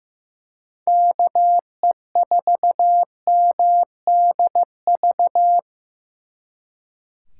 Morse code, heard as a keyed tone, about 18 wpm, 700 Hz. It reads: KE4MDV
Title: Speed Racing - USA 2x3 Call Signs 15wpm